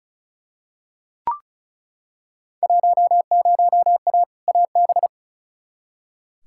Morse code, heard as a keyed tone, about 36 wpm, 690 Hz. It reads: 10A AB
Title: Field Day Exchanges Encore - 35wpm